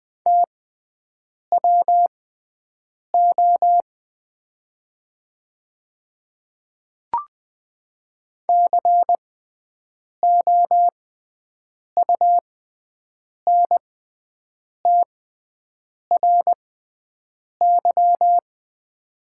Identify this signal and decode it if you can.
Morse code, heard as a keyed tone, about 20 wpm, spaced as for 7 wpm, 700 Hz. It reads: TWO COUNTRY
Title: 6x Character Spacing - Top 300 Words - 20wpm